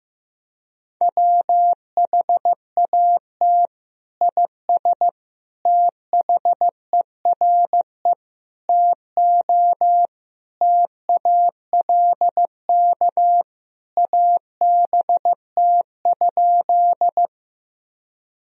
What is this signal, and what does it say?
Morse code, heard as a keyed tone, about 15 wpm, 700 Hz. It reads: WHAT IS THERE TO TALK ABT?
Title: Sentences from Top 400 Words and Abbreviations 15wpm